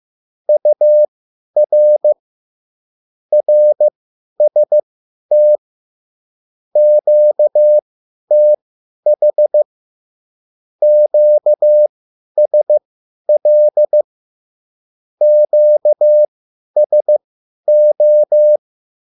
Morse code, heard as a keyed tone, about 15 wpm, spaced as for 10 wpm, 600 Hz. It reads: UR RST QTH QSL QSO